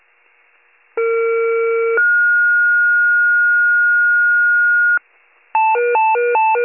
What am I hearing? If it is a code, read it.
Two-tone page, heard as a one-second tone, then a three-second tone, about 460 then 1430 Hz. It is heard from 1.0 to 5.0 s.